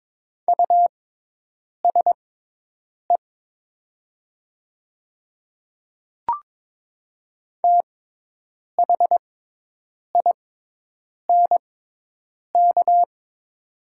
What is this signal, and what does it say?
Morse code, heard as a keyed tone, about 23 wpm, spaced as for 8 wpm, 700 Hz. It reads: USE THINK
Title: 6x Character Spacing - Top 500 Words - 22wpm